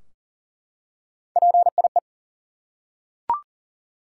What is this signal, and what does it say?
Morse code, heard as a keyed tone, about 38 wpm, 700 Hz. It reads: PIE
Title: Three Letter Words - 40wpm